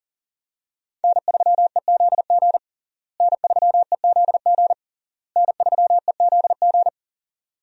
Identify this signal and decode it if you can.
Morse code, heard as a keyed tone, about 40 wpm, 700 Hz. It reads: N3EZG N3EZG N3EZG
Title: USA Call Signs - 5 Characters - Repeated 3x - 40wpm